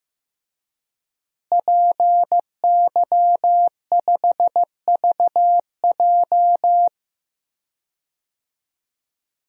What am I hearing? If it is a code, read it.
Morse code, heard as a keyed tone, about 15 wpm, 700 Hz. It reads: PY5VJ
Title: DXCC-100 Call Signs - 5 Characters - Repeated 3x - 15wpm